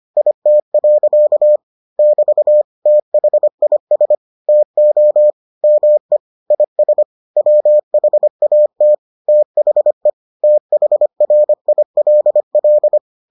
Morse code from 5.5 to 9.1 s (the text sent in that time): ME IS WHAT